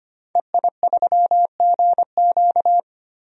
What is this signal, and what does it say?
Morse code, heard as a keyed tone, about 25 wpm, 700 Hz. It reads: EI3GQ
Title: DXCC-100 Call Signs - 5 Characters - 25wpm